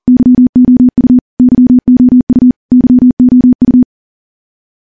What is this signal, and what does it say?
Morse code, heard as a keyed tone, about 41 wpm, 250 Hz. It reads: YOU YOU YOU
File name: morse-training-40WPM